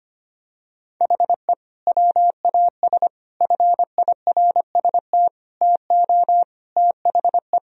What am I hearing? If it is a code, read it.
Morse code, heard as a keyed tone, about 25 wpm, 700 Hz. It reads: HE WAS FIRST TO THE
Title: Sentences from Top 1000 Words - Review Part 2 - 25wpm